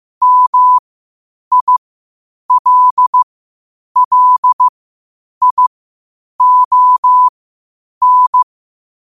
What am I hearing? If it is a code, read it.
Morse code, heard as a keyed tone, about 15 wpm, spaced as for 8 wpm, 1000 Hz. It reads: MILLION